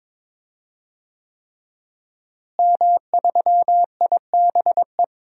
Morse code, heard as a keyed tone, about 22 wpm, 700 Hz. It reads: M3IBE
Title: DXCC-100 Call Signs - 5 Characters - Repeated 3x - 22wpm